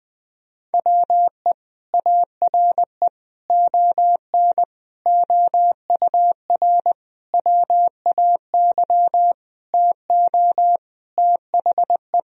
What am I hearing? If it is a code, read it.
Morse code, heard as a keyed tone, about 21 wpm, 700 Hz. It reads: WE ARE ON OUR WAY TO THE
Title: Sentences from Top 1000 Words - Review Part 2 - 20wpm